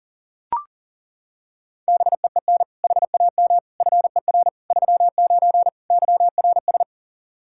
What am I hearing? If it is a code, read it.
Morse code, heard as a keyed tone, about 40 wpm, 700 Hz. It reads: BEEN HAM FER 39 YRS
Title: K1USN SST QSO Practice 40wpm